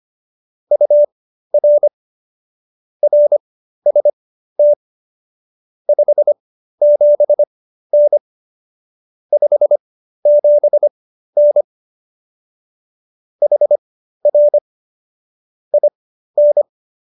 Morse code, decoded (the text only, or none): UR RST 57N 57N HR IN